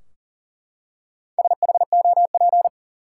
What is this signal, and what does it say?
Morse code, heard as a keyed tone, about 40 wpm, 700 Hz. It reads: SHOP
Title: Four Letter Words - 40wpm